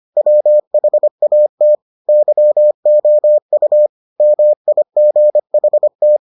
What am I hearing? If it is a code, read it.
Morse code, heard as a keyed tone, about 25 wpm, 600 Hz. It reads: WHAT YOU MIGHT